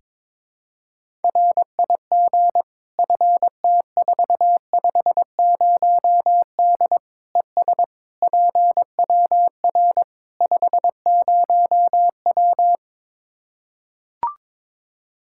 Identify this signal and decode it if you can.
Morse code, heard as a keyed tone, about 22 wpm, 700 Hz. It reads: RIG FT450D ES PWR 50W
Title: K1USN SST QSO Practice 22wpm